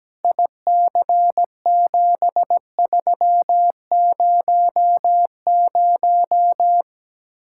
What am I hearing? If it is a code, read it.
Morse code, heard as a keyed tone, about 17 wpm, 700 Hz. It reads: IC7300
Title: Speed-Racing Common QSO Phrases Encore - 17wpm